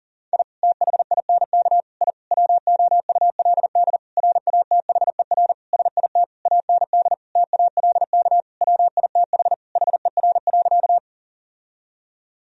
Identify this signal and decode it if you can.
Morse code, heard as a keyed tone, about 38 wpm, 700 Hz. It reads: I THINK I WOULD RATHER SIT AND TALK WITH HER.
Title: Sentences from Top 1000 Words - Review Part 4 - 40wpm